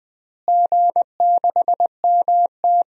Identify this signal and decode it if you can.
Morse code, heard as a keyed tone, about 21 wpm, 700 Hz. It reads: G6MT
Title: DXCC-100 Call Signs - 4 Characters - Repeated 3x - 20wpm